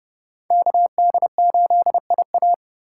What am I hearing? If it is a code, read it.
Morse code, heard as a keyed tone, about 40 wpm, 700 Hz. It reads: KD8IA